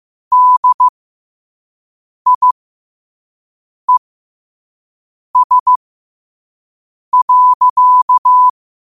Morse code, heard as a keyed tone, about 15 wpm, spaced as for 5 wpm, 1000 Hz. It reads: DIES.
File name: qotdfiles / 5wpm